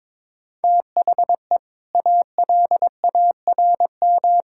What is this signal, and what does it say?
Morse code, heard as a keyed tone, about 21 wpm, 700 Hz. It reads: THE ALARM
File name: practice / 22wpm / Easy Sentences with Hints - 22wpm